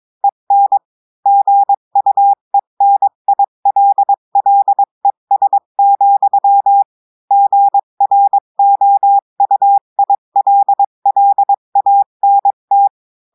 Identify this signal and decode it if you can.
Morse code, heard as a keyed tone, about 22 wpm, 800 Hz. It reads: EN GUENILLES, GROUILLANT
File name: cours-du-mercredi / audios / cours-853-du-2-mai-2024-etxte chiffre et texte clair-a-20-wpm